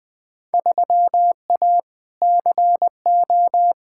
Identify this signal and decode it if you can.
Morse code, heard as a keyed tone, about 20 wpm, 700 Hz. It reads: 3A CO
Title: Field Day Exchanges Encore - 20wpm